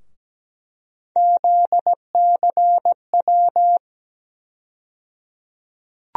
Morse code, heard as a keyed tone, about 17 wpm, 700 Hz. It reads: ZCW
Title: USA Call Sign Postfixes - 3 Letters 17wpm